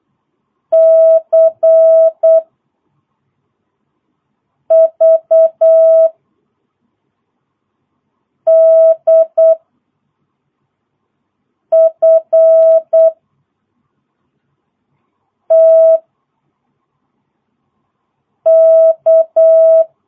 Morse code, heard as a keyed tone, about 8 wpm, 660 Hz. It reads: C V D F T K
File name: jh8cbh / 2022cw